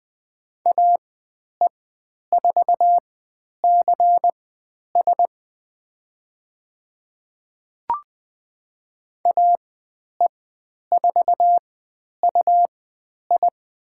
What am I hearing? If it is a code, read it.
Morse code, heard as a keyed tone, about 20 wpm, spaced as for 10 wpm, 700 Hz. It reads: AE4CS AE4UI
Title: 109 Lesson 18g - Letter m - Call Signs 20wpm